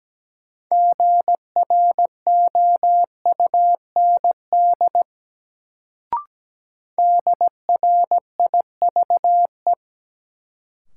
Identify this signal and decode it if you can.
Morse code, heard as a keyed tone, about 17 wpm, 700 Hz. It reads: GROUND DRIVE
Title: Sentences from Top 500 Words - New Word Drill - 17wpm